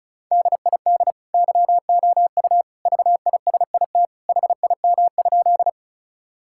Morse code, heard as a keyed tone, about 36 wpm, 700 Hz. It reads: DID YOU VISIT HIM?
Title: Sentences from Top 1000 Words - Review Part 6 - 35wpm